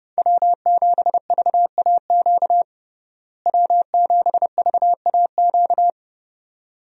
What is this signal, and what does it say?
Morse code, heard as a keyed tone, about 30 wpm, 700 Hz. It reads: W7VAQ W7VAQ
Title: USA Call Signs - 5 Characters - Repeated 2x - 30wpm